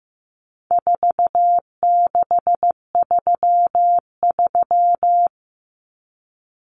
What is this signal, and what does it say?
Morse code, heard as a keyed tone, about 15 wpm, 700 Hz. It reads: 4633
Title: Common Cut Numbers - 4-Digit 15wpm